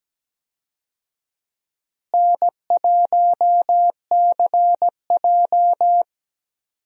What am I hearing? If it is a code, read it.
Morse code, heard as a keyed tone, about 17 wpm, 700 Hz. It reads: N1CJ
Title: USA Call Signs - 4 Characters - Repeated 2x - 17wpm